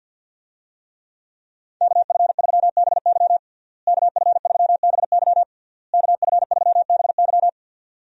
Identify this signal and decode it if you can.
Morse code, heard as a keyed tone, about 52 wpm, 700 Hz. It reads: KF3BY KF3BY KF3BY
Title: USA Call Signs - 5 Characters - Repeated 3x - 50wpm